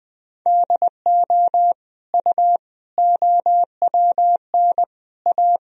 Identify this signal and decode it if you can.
Morse code, heard as a keyed tone, about 21 wpm, 700 Hz. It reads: DO U OWN A
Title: Sentences from Top 400 Words and Abbreviations 20wpm